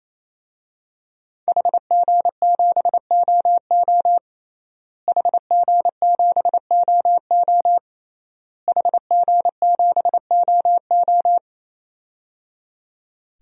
Morse code, heard as a keyed tone, about 28 wpm, 700 Hz. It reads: HG7OO HG7OO HG7OO
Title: DXCC-100 Call Signs - 5 Characters - Repeated 3x - 28wpm